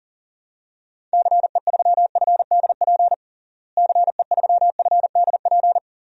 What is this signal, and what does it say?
Morse code, heard as a keyed tone, about 40 wpm, 700 Hz. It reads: CE3FDP CE3FDP
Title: DXCC-100 Call Signs - 6 Characters - Repeated 2x - 40wpm